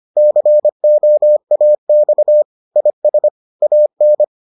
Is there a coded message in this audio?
Yes: Morse code from 0.2 to 4.2 s, heard as a keyed tone, about 25 wpm, 600 Hz. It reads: COAX IS AN